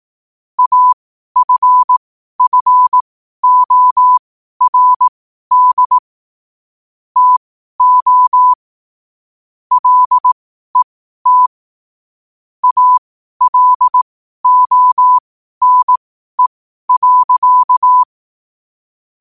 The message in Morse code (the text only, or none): AFFORD TO LET ALONE.